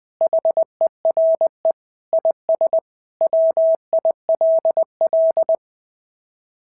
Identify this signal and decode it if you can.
Morse code, heard as a keyed tone, about 21 wpm, 690 Hz. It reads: HERE IS WILL
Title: SOTA QSO Practice 20wpm